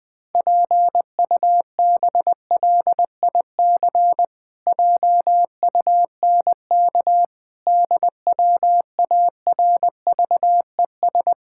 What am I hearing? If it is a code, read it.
Morse code, heard as a keyed tone, about 20 wpm, 700 Hz. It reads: PUBLIC JUNK DWARVES
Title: Pangram Challenge v2 - 20wpm